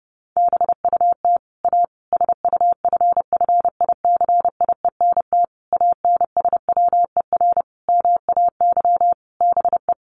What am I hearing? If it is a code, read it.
Morse code, heard as a keyed tone, about 28 wpm, 700 Hz. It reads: BUT A SUFFICIENT ANSWER MAY BE
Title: Sentences from Top 1400 Words 30wpm